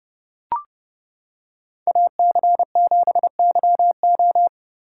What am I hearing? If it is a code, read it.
Morse code, heard as a keyed tone, about 29 wpm, 700 Hz. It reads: AC7YO